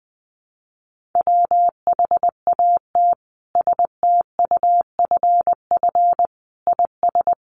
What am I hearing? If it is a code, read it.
Morse code, heard as a keyed tone, about 20 wpm, 700 Hz. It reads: WHAT STUFF IS
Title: Sentences from Top 1400 Words 20wpm